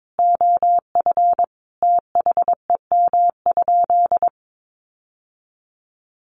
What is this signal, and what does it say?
Morse code, heard as a keyed tone, about 22 wpm, 700 Hz. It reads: OF THEM?
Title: Sentences from Top 600 Words 22wpm